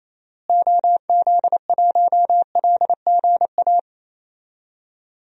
Morse code, heard as a keyed tone, about 28 wpm, 700 Hz. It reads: OZ1LGA
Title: DXCC-100 Call Signs - 6 Characters - Repeated 2x - 28wpm